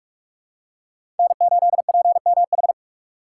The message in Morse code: N8PMH

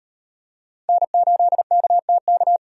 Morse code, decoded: N8KTK

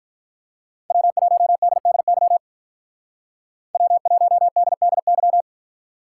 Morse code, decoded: W1DDY W1DDY

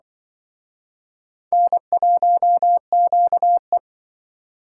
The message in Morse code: N1QE

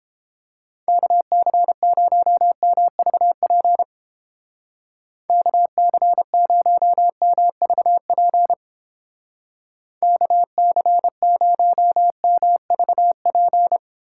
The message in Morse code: KC0MVP KC0MVP KC0MVP

